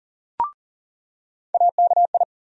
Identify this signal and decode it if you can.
Morse code, heard as a keyed tone, about 40 wpm, 700 Hz. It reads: AKI
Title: USA Call Sign Postfixes - 3 Letters 40wpm